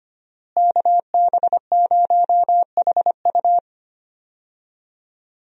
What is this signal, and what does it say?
Morse code, heard as a keyed tone, about 25 wpm, 700 Hz. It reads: KB0HU
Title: USA Call Signs - 5 Characters - Repeated 2x - 25wpm